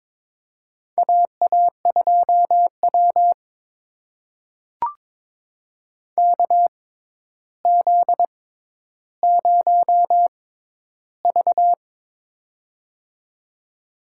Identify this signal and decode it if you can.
Morse code, heard as a keyed tone, about 22 wpm, spaced as for 11 wpm, 700 Hz. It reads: AA2W KZ0V
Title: USA Call Signs - 4 Characters - 6x Character Spacing - 22wpm